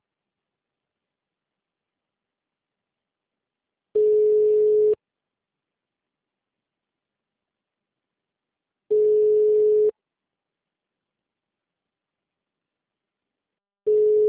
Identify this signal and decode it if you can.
Ringback tone (Europe, 425 Hz).